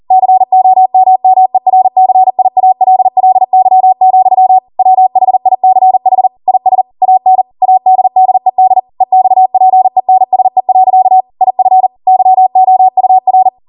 Morse code, heard as a keyed tone, about 40 wpm, 750 Hz. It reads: COMMERCIALLY, WHICH IS AN ADDED EXPENSE. IF YOUR